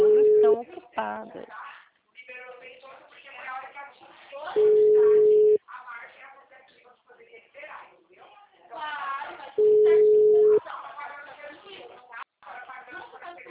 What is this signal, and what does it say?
Ringback tone (Europe, 425 Hz), with speech.